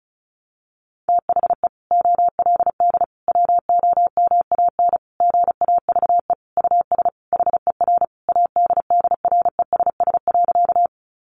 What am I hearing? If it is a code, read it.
Morse code, heard as a keyed tone, about 34 wpm, 700 Hz. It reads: THE OLD WOMAN GAVE US HER ADDRESS.
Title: Sentences from Top 1100 Words 35wpm